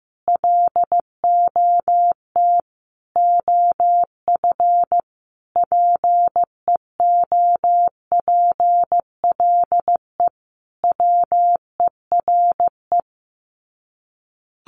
Morse code, heard as a keyed tone, about 14 wpm, 700 Hz. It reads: LOT OF PEOPLE WERE
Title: Sets of 5 Words Encore2 15wpm